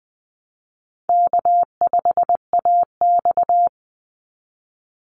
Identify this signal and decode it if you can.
Morse code, heard as a keyed tone, about 20 wpm, 700 Hz. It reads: K5AX
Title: Easy Call Signs - 20wpm